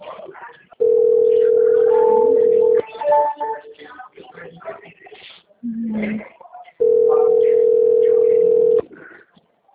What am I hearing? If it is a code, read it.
Ringback tone (North America), 440 + 480 Hz.